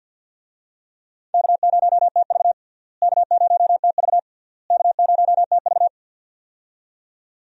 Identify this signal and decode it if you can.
Morse code, heard as a keyed tone, about 48 wpm, 700 Hz. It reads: K0TV K0TV K0TV
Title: USA Call Signs - 4 Characters - Repeated 3x - 50wpm